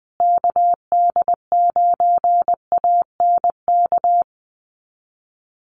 Morse code, heard as a keyed tone, about 20 wpm, 700 Hz. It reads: KD9ANK
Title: USA Call Signs Repeated 3x 20wpm